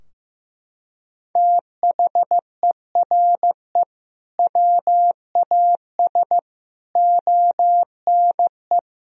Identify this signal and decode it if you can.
Morse code, heard as a keyed tone, about 15 wpm, 700 Hz. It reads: THERE WAS ONE
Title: Sentences from Top 1000 Words - Review Part 3 - 15wpm